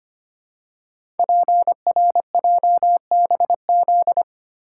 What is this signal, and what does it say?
Morse code, heard as a keyed tone, about 25 wpm, 700 Hz. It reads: PRJBZ
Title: Code Groups - 5 Letters - 25wpm